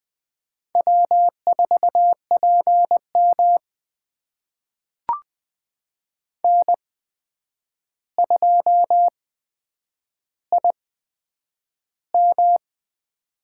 Morse code, heard as a keyed tone, about 20 wpm, spaced as for 9 wpm, 700 Hz. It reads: W4PM N2IM
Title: USA Call Signs - 4 Characters - 8x Character Spacing - 20wpm